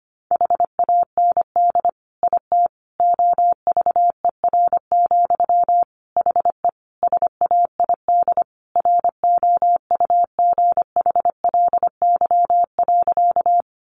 Morse code, heard as a keyed tone, about 25 wpm, 700 Hz. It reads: HAND IT OVER, HE SAID ROUGHLY.
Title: Sentences from Top 1500 Words 25wpm